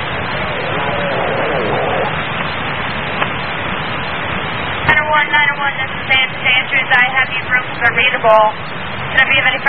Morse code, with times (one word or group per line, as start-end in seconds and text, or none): none